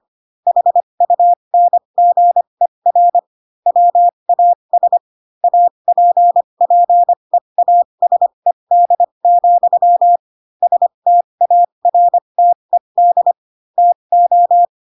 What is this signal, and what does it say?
Morse code, heard as a keyed tone, about 25 wpm, spaced as for 22 wpm, 700 Hz. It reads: HUNGER WAS APPEASED, STARTED TO
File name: Pinocchio / Pinocchio ch8 18 25 ik1hge